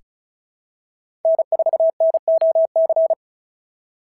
Morse code, heard as a keyed tone, about 36 wpm, 660 Hz. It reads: N4NOC
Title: SOTA QSO Practice 35wpm